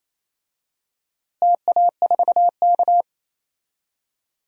Morse code, heard as a keyed tone, about 28 wpm, 700 Hz. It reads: TA4K
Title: DXCC-100 Call Signs - 4 Characters - Repeated 2x - 28wpm